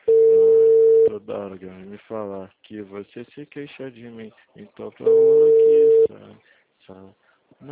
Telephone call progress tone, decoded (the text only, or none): ringback tone